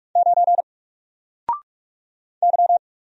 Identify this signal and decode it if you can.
Morse code, heard as a keyed tone, about 44 wpm, 700 Hz. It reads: 9 Y
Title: Single Letter-Number 45wpm